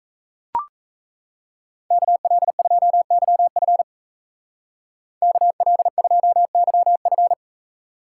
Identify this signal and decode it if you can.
Morse code, heard as a keyed tone, about 33 wpm, 700 Hz. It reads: KL2YF KL2YF
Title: Speed Racing - USA 2x2 Call Signs 28wpm